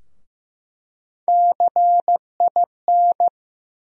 Morse code, heard as a keyed tone, about 15 wpm, 700 Hz. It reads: CIN
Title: USA Call Sign Postfixes - 3 Letters 15wpm